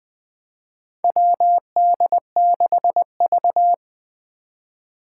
Morse code, heard as a keyed tone, about 21 wpm, 700 Hz. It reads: WD6V
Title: Four Challenging Letter-Number Combinations 20wpm